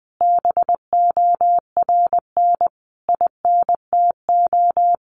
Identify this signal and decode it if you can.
Morse code, heard as a keyed tone, about 20 wpm, 700 Hz. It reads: BORN INTO
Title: Sentences from Top 1400 Words 20wpm